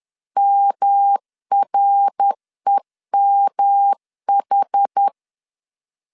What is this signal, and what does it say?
Morse code, heard as a keyed tone, about 10 wpm, 800 Hz. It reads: MREMH
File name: CW / DET / 1 3 1500C S C 10wpm 3335